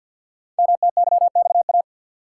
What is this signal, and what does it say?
Morse code, heard as a keyed tone, about 52 wpm, 700 Hz. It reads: MTYXA
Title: Code Groups - 5 Letters - 50wpm